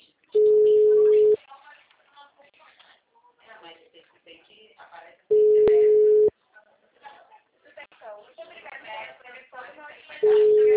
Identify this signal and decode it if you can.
Ringback tone (Europe, 425 Hz), with speech.